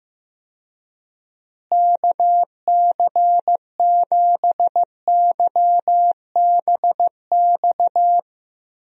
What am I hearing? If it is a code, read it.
Morse code, heard as a keyed tone, about 15 wpm, 700 Hz. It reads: KC7YBX
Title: USA Call Signs - 6 Characters - Repeated 3x - 15wpm